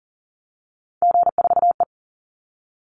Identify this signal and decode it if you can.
Morse code, heard as a keyed tone, about 40 wpm, 700 Hz. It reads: G4E
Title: Call Signs - 40wpm